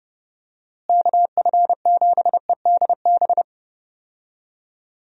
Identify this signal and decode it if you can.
Morse code, heard as a keyed tone, about 30 wpm, 700 Hz. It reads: KF7EDB
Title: USA Call Signs - 6 Characters - Repeated 2x - 30wpm